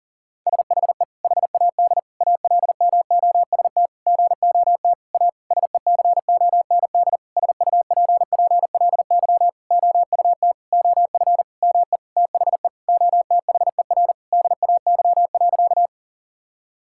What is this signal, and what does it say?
Morse code, heard as a keyed tone, about 40 wpm, 700 Hz. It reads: SHE HAD ALMOST GOT A SECOND SUPPLY OUT OF ME THE OTHER DAY.
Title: Sentences from Top 1000 Words - Review Part 5 - 40wpm